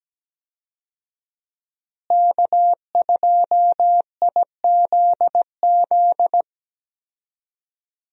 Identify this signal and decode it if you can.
Morse code, heard as a keyed tone, about 17 wpm, 700 Hz. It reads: K2IZZ
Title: USA Call Signs - 5 Characters - Repeated 2x - 17wpm